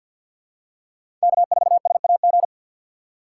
Morse code, heard as a keyed tone, about 53 wpm, 700 Hz. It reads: K4SAG
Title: USA Call Signs - 5 Characters - 8x Character Spacing - 50wpm